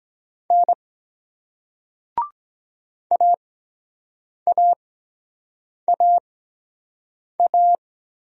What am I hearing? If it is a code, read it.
Morse code, heard as a keyed tone, about 23 wpm, spaced as for 7 wpm, 700 Hz. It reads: N AAAA